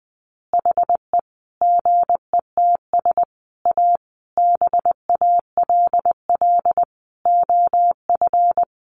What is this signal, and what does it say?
Morse code, heard as a keyed tone, about 20 wpm, 700 Hz. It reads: HE GETS A BALL OF
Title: Additional Easy Phrases 20wpm